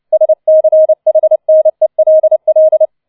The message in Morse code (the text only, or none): SCHNELL